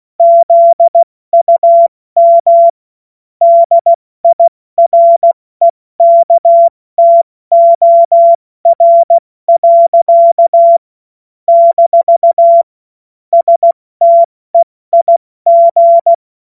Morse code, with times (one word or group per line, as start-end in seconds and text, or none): 0.2-2.7 ZUM
3.4-10.8 DIREKTOR.
11.5-12.6 -
13.3-16.1 STEIG